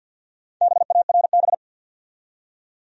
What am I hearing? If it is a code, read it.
Morse code, heard as a keyed tone, about 46 wpm, 700 Hz. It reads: BARB
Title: USA Common Names - Max 5 Letters - 50wpm